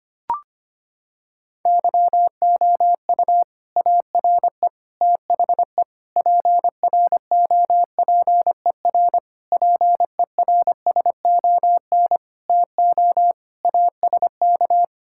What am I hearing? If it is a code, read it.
Morse code, heard as a keyed tone, about 25 wpm, 700 Hz. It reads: YOU ARE THE PROPER PERSON TO ASK
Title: Sentences from Top 1000 Words - Review Part 5 - 25wpm